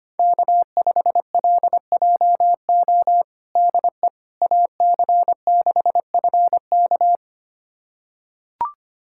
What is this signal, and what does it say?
Morse code, heard as a keyed tone, about 25 wpm, 700 Hz. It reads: K5LJO DE AC6FK
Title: USA Callsign DE Callsign - 5 Characters - 25wpm